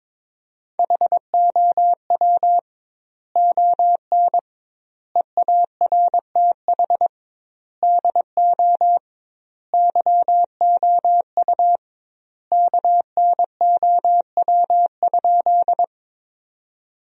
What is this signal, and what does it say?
Morse code, heard as a keyed tone, about 22 wpm, 700 Hz. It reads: HOW ON EARTH DO YOU KNOW?
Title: 2x Word-Spacing - Sentences from Top 200 Words - 22wpm